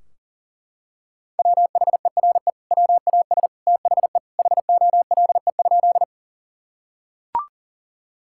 Morse code, heard as a keyed tone, about 40 wpm, 700 Hz. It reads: WHERE WAS THE HOLE?
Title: Sentences from Top 1600 Words 40wpm